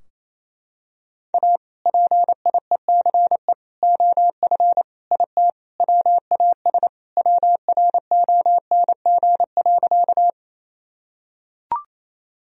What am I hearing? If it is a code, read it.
Morse code, heard as a keyed tone, about 27 wpm, 700 Hz. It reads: A PIECE OF IT WAS WRONG.
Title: Sentences from Top 1000 Words - Review Part 2 - 28wpm